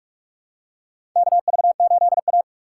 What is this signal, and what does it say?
Morse code, heard as a keyed tone, about 47 wpm, 700 Hz. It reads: KV8A
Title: USA Call Signs - 4 Characters - 6x Character Spacing - 45wpm